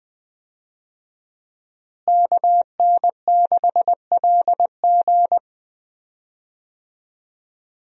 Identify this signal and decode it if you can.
Morse code, heard as a keyed tone, about 20 wpm, 700 Hz. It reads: KN6LG
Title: USA Call Signs - 5 Characters - Repeated 2x - 20wpm